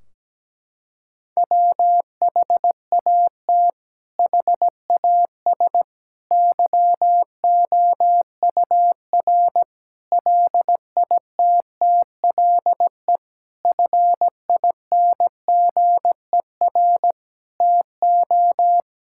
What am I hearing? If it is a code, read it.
Morse code, heard as a keyed tone, about 17 wpm, 700 Hz. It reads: WHAT HAS YOUR LITTLE FINGER TO
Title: Sentences from Top 1600 Words 17wpm